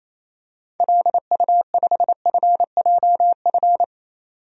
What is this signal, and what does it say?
Morse code, heard as a keyed tone, about 28 wpm, 700 Hz. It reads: LU5FJF
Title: DXCC-100 Call Signs - 6 Characters - 28wpm